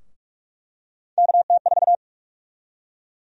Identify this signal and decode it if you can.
Morse code, heard as a keyed tone, about 46 wpm, 700 Hz. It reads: KT4
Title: USA Call Sign Prefixes - 45wpm